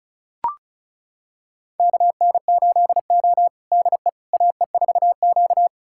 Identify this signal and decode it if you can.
Morse code, heard as a keyed tone, about 34 wpm, 700 Hz. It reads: KN8O DE AE4Q
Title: USA Callsign DE Callsign - 4 Characters - 35wpm